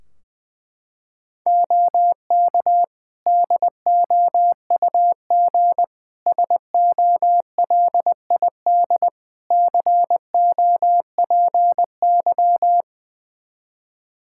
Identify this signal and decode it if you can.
Morse code, heard as a keyed tone, about 20 wpm, 700 Hz. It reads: OK DOUG SOLID COPY
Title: Practice QSO - Exchange 3 - More Info - 20wpm